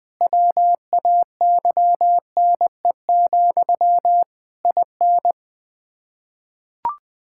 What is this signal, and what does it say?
Morse code, heard as a keyed tone, about 20 wpm, 700 Hz. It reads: WAYNE, IN
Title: Practice QSO - Exchange 1 - Beginning - 20wpm